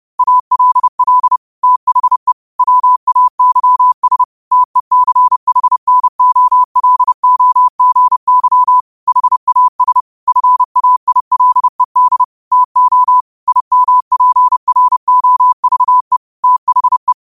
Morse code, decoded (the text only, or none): ALL THE WAYS TECHNOLOGY HAS FAILED TO IMPROVE THE